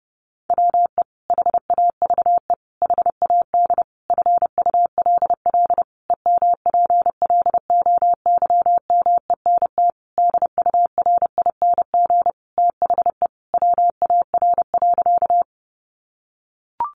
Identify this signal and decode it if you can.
Morse code, heard as a keyed tone, about 30 wpm, 700 Hz. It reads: WE HAVE HAD FULL EMPLOYMENT DURING THE WAR.
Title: Sentences from Top 1500 Words 30wpm